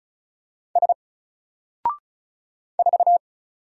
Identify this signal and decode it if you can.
Morse code, heard as a keyed tone, about 36 wpm, 700 Hz. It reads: S 4
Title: Single Letter-Number Alt 35wpm